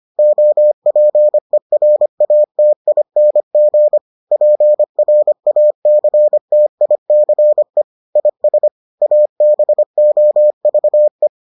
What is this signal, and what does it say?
Morse code, heard as a keyed tone, about 25 wpm, 600 Hz. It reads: OPERATING PRACTICE IS ABOVE